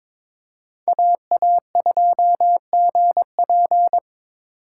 Morse code, heard as a keyed tone, about 21 wpm, 700 Hz. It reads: AA2GP
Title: USA Call Signs - 4x Character Spacing - 22wpm